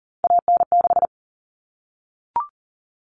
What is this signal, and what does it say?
Morse code, heard as a keyed tone, about 40 wpm, 700 Hz. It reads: AN6
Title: All Cut Numbers - 3-Digit 40wpm